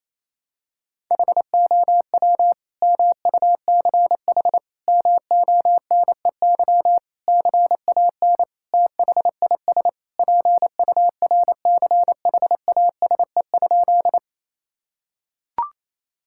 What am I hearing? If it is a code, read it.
Morse code, heard as a keyed tone, about 28 wpm, 700 Hz. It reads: HOW MUCH MONEY CAN THIS PURCHASE?
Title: Sentences from Top 1000 Words - Review Part 5 - 28wpm